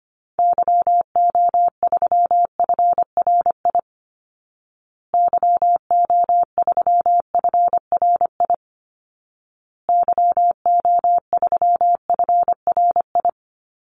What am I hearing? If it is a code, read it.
Morse code, heard as a keyed tone, about 25 wpm, 700 Hz. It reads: YO3FRI YO3FRI YO3FRI
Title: Call Signs Repeated 2x - 25wpm